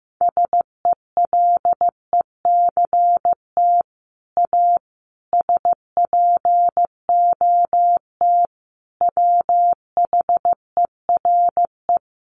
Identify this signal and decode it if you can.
Morse code, heard as a keyed tone, about 15 wpm, 700 Hz. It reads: SELECT A SPOT WHERE
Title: Sentences from Top 1300 Words 15wpm